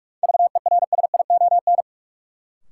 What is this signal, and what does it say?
Morse code, heard as a keyed tone, about 42 wpm, 700 Hz. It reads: VERSION
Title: Sentences from Top 1400 Words - New Word Drill Alt - 45wpm